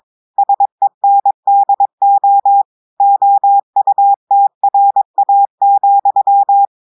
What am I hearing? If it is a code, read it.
Morse code, heard as a keyed tone, about 22 wpm, 800 Hz. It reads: SENDO OUTRA,
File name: audios / portugais / portugais texte en clair 1 vitesse 22 mots minutes